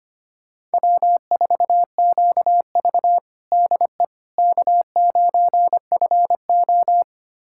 Morse code, heard as a keyed tone, about 25 wpm, 700 Hz. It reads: W4QV DE K9FO
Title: USA Callsign DE Callsign - 4 Characters - 25wpm